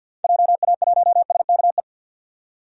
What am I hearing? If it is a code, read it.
Morse code, heard as a keyed tone, about 51 wpm, 700 Hz. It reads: JA1SKE
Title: DXCC-100 Call Signs - 6 Characters - 50wpm